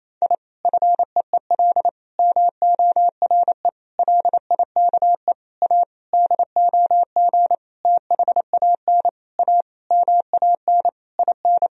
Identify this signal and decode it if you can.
Morse code, heard as a keyed tone, about 27 wpm, 700 Hz. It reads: I FEEL MORE LIKE A DOG THAN A MAN IN
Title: Sentences from Top 1000 Words - Review Part 2 - 28wpm